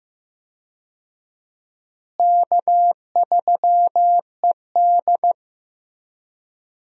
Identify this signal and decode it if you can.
Morse code, heard as a keyed tone, about 16 wpm, 700 Hz. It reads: K3ED
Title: USA Call Signs - 4 Characters - Repeated 3x - 15wpm